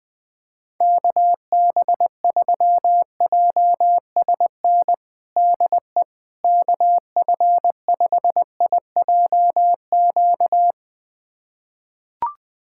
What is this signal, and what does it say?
Morse code, heard as a keyed tone, about 19 wpm, 700 Hz. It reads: KB3JSN DE KF5IJQ
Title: USA Callsign DE Callsign - 6 Characters - 20wpm